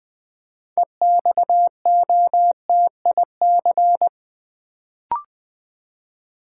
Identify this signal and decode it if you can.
Morse code, heard as a keyed tone, about 20 wpm, 700 Hz. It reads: EXOTIC
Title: Build A Word - 3 Letters At A Time - 6-Letter Words - 20wpm